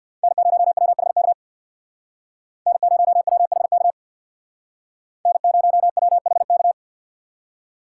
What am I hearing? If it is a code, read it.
Morse code, heard as a keyed tone, about 62 wpm, 700 Hz. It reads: N0WHK N0WHK N0WHK
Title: Speed Racing - USA 1x3 Call Signs 50wpm